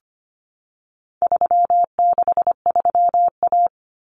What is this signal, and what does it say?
Morse code, heard as a keyed tone, about 25 wpm, 700 Hz. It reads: 363A
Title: Common Cut Numbers - 4-Digit 25wpm